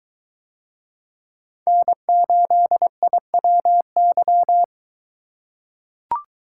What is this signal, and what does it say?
Morse code, heard as a keyed tone, about 21 wpm, 700 Hz. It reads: N8IWY